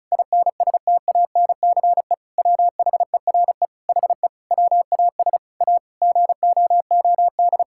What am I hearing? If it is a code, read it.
Morse code, heard as a keyed tone, about 36 wpm, 700 Hz. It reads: INSTANCE WHERE HE WAS A GOOD
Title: Sentences from Top 1000 Words - Review Part 5 - 35wpm